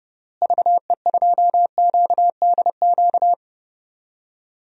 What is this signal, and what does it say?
Morse code, heard as a keyed tone, about 29 wpm, 700 Hz. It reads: VE2QDQ
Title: DXCC-100 Call Signs - 6 Characters - Repeated 2x - 30wpm